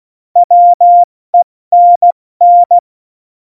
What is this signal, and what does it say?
Morse code, heard as a keyed tone, about 16 wpm, 700 Hz. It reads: WENN
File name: Morsekurs / Witze / Witz 06. November 2018, 70 F80